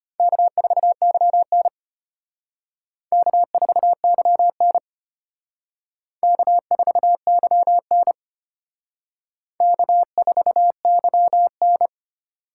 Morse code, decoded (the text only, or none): K4YN K4YN K4YN K4YN